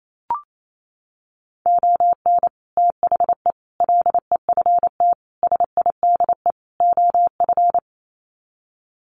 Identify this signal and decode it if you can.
Morse code, heard as a keyed tone, about 28 wpm, 700 Hz. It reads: ON THE LEFT SIDE OF
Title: Sets of 5 Words Encore2 28wpm